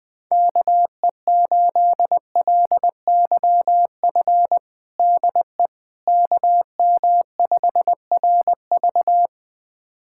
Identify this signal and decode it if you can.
Morse code, heard as a keyed tone, about 21 wpm, 700 Hz. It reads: KE8LYF DE KM5RV
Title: USA Callsign DE Callsign 20wpm